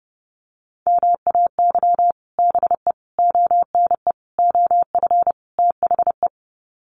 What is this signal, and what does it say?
Morse code, heard as a keyed tone, about 30 wpm, 700 Hz. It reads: MAY BE ONE OF THE
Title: Sets of 5 Words Encore2 30wpm